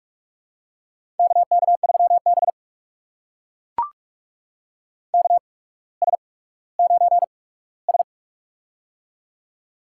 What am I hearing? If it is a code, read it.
Morse code, heard as a keyed tone, about 44 wpm, spaced as for 19 wpm, 700 Hz. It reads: KK3B KS9S
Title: USA Call Signs - 4 Characters - 8x Character Spacing - 45wpm